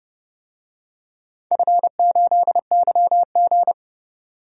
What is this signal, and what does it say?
Morse code, heard as a keyed tone, about 30 wpm, 700 Hz. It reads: F8YG
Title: DXCC-100 Call Signs - 4 Characters - 30wpm